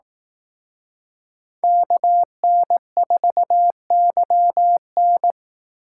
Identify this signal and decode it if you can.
Morse code, heard as a keyed tone, about 18 wpm, 700 Hz. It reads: KN4YN
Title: Speed Racing - USA Call Signs 15wpm